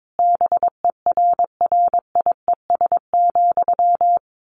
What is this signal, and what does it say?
Morse code, heard as a keyed tone, about 22 wpm, 700 Hz. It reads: BERRIES,
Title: A Sound Of Thunder 22wpm